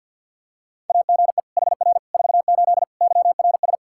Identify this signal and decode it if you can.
Morse code, heard as a keyed tone, about 49 wpm, 700 Hz. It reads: AGE HR 48 YRS
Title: Practice QSO - Exchange 3 - More Info - 50wpm